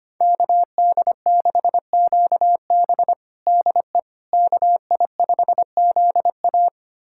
Morse code, heard as a keyed tone, about 25 wpm, 700 Hz. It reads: KD6QB DE KI5ZA
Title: USA Callsign DE Callsign - 5 Characters - 25wpm